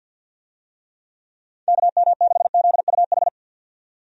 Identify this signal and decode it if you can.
Morse code, heard as a keyed tone, about 45 wpm, 700 Hz. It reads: KM6ZUH
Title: USA Call Signs - 6 Characters - 50wpm